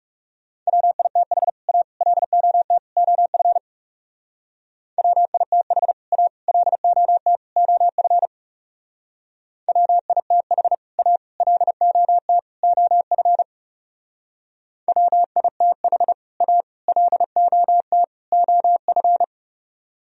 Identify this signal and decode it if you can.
Morse code, heard as a keyed tone, about 39 wpm, 700 Hz. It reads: WITH A LOT OF WITH A LOT OF WITH A LOT OF WITH A LOT OF